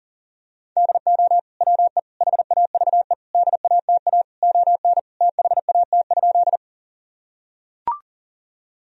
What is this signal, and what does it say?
Morse code, heard as a keyed tone, about 40 wpm, 700 Hz. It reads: DO WE HAVE DATA ON THAT?
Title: Sentences from Top 1000 Words - Review Part 3 - 40wpm